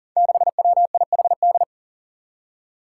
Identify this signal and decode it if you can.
Morse code, heard as a keyed tone, about 40 wpm, 700 Hz. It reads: 6WIHD
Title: Five Challenging Letter-Number Combinations 40wpm